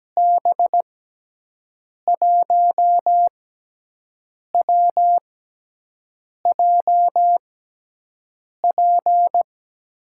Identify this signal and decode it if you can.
Morse code, heard as a keyed tone, about 17 wpm, spaced as for 6 wpm, 700 Hz. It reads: B1WJP
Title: USA Call Signs - 6 Characters - 6x Character Spacing - 17wpm